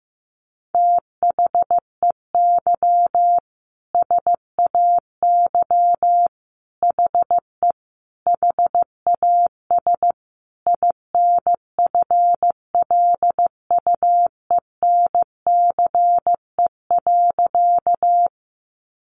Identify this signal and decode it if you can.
Morse code, heard as a keyed tone, about 15 wpm, 700 Hz. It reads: THEY SAY HE HAS INFLUENCE.
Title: Sentences from Top 1100 Words 15wpm